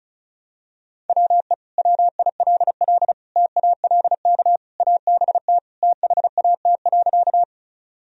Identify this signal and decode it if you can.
Morse code, heard as a keyed tone, about 36 wpm, 700 Hz. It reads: WE WILL TALK ABT THAT.
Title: Sentences from Top 400 Words and Abbreviations 35wpm